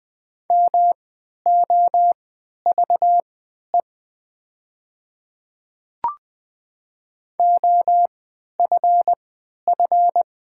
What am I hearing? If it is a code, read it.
Morse code, heard as a keyed tone, about 20 wpm, spaced as for 11 wpm, 700 Hz. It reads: MOVE OFF
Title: 3x Character Spacing - Top 200 Words - 20wpm